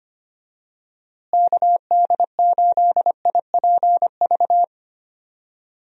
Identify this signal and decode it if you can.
Morse code, heard as a keyed tone, about 25 wpm, 700 Hz. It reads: KD8IPV
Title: USA Call Signs - 6 Characters - Repeated 2x - 25wpm